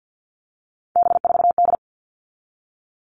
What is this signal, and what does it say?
Morse code, heard as a keyed tone, about 50 wpm, 700 Hz. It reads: B4D
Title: All Cut Numbers - 3-Digit 50wpm